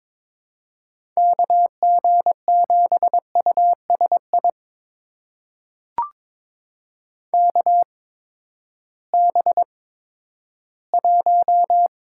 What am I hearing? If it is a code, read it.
Morse code, heard as a keyed tone, about 22 wpm, spaced as for 12 wpm, 700 Hz. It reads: KG7USI KB1
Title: USA Call Signs - 6 Characters - 8x Character Spacing - 22wpm